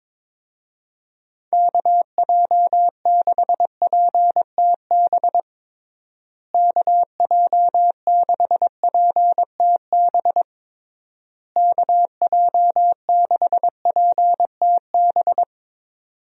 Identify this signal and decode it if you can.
Morse code, heard as a keyed tone, about 22 wpm, 700 Hz. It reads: KJ6PTB KJ6PTB KJ6PTB
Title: USA Call Signs - 6 Characters - Repeated 3x - 22wpm